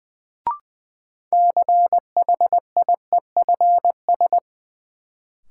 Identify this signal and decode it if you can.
Morse code, heard as a keyed tone, about 20 wpm, 700 Hz. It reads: CHIEFS
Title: Sets of 5 Words Encore - New Word Drill - 20wpm